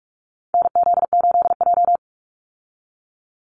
Morse code, heard as a keyed tone, about 46 wpm, 700 Hz. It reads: NZ8J
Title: Call Signs - 45wpm